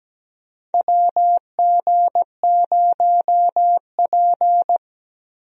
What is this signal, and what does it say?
Morse code, heard as a keyed tone, about 17 wpm, 700 Hz. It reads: WG0P
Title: USA Call Signs - 4 Characters - Repeated 3x - 17wpm